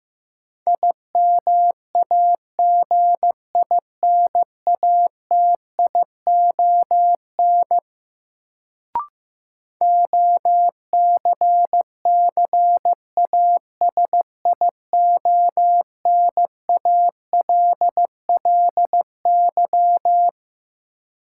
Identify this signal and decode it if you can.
Morse code, heard as a keyed tone, about 15 wpm, 700 Hz. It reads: IMAGINATION OCCASIONALLY
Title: Sentences from Top 1300 Words - New Word Drill Alt - 15wpm